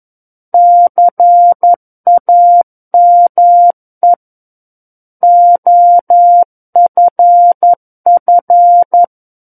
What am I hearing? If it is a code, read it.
Morse code, heard as a keyed tone, about 11 wpm, 700 Hz. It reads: CAME OFF
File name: Book One / 02-Chapter 2 - 11 WPM